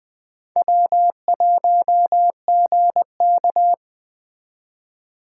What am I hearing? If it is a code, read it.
Morse code, heard as a keyed tone, about 20 wpm, 700 Hz. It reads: W1GK
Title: Speed Racing - USA 1x2 Call Signs 17wpm